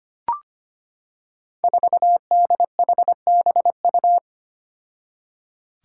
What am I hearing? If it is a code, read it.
Morse code, heard as a keyed tone, about 25 wpm, 700 Hz. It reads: 4DHBU
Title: Five Challenging Letter-Number Combinations 25wpm